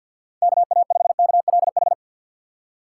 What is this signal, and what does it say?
Morse code, heard as a keyed tone, about 47 wpm, 700 Hz. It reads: KA5KLH
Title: USA Call Signs - 6 Characters - 4x Character Spacing - 50wpm